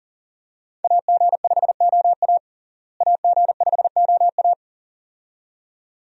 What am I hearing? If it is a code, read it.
Morse code, heard as a keyed tone, about 40 wpm, 700 Hz. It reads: AG5OA AG5OA
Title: USA Call Signs - 5 Characters - Repeated 2x - 40wpm